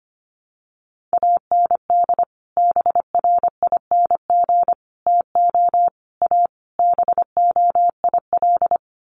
Morse code, heard as a keyed tone, about 25 wpm, 700 Hz. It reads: AND BRING TO A BOIL
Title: Sets of 5 Words Encore 25wpm